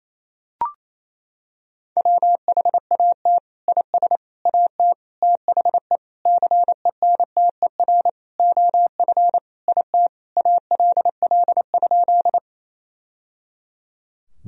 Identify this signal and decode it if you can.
Morse code, heard as a keyed tone, about 27 wpm, 700 Hz. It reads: WHAT IS AT THE CENTER OF IT ALL?
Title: Sentences from Top 1000 Words - Review Part 2 - 28wpm